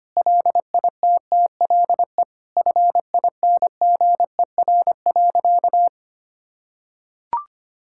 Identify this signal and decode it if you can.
Morse code, heard as a keyed tone, about 25 wpm, 700 Hz. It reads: LITTLE FINGER.
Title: Sentences from Top 1600 Words 25wpm